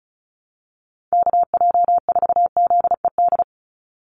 Morse code, heard as a keyed tone, about 34 wpm, 700 Hz. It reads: KJ4ZED
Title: USA Call Signs 35wpm